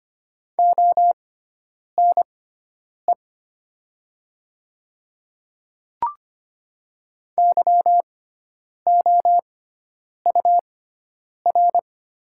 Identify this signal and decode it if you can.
Morse code, heard as a keyed tone, about 25 wpm, spaced as for 9 wpm, 700 Hz. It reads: ONE YOUR
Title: 6x Character Spacing - Top 100 Words - 25wpm